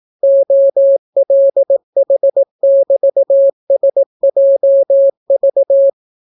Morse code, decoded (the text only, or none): OLH<BT>SJV